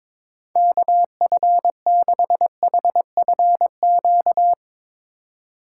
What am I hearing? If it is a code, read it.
Morse code, heard as a keyed tone, about 22 wpm, 700 Hz. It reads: KF6HFQ
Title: USA Call Signs - 6 Characters - 3x Character Spacing - 22wpm